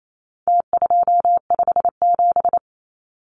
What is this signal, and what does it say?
Morse code, heard as a keyed tone, about 28 wpm, 700 Hz. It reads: T257
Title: Numbers - Contest Style - 4-Digit 28wpm